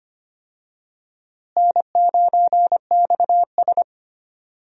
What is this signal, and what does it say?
Morse code, heard as a keyed tone, about 25 wpm, 700 Hz. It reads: N9XS
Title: USA Call Signs - 4 Characters - Repeated 3x - 25wpm